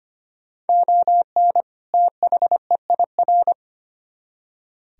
Morse code, heard as a keyed tone, about 33 wpm, 700 Hz. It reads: ON THEIR